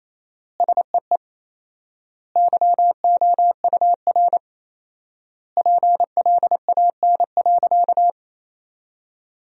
Morse code, heard as a keyed tone, about 28 wpm, 700 Hz. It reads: SEE YOUR PLAN.
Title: 4x Word-Spacing - Sentences from Top 500 Words - 28wpm